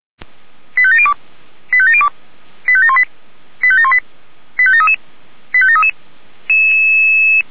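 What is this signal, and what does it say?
Five-tone selective call (ZVEI), heard as quick runs of five tones, tones of about 70 ms. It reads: 85702 85702 85718 85718 85733 85733
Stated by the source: ZVEI five-tone selective call received by SDR, fire brigade (Feuerwehr) Orlamünde.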